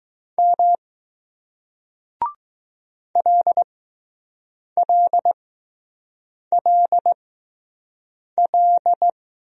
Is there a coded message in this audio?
Yes: Morse code from 0.4 to 9.1 s, heard as a keyed tone, about 20 wpm, spaced as for 6 wpm, 700 Hz. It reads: M LLLL